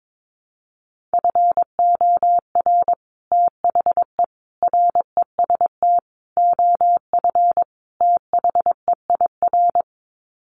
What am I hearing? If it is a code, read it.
Morse code, heard as a keyed tone, about 22 wpm, 700 Hz. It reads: FOR THE REST OF THEIR
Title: Sets of 5 Words Encore2 22wpm